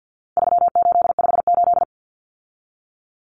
Morse code, heard as a keyed tone, about 50 wpm, 700 Hz. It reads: 3858
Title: Common Cut Numbers - 4-Digit 50wpm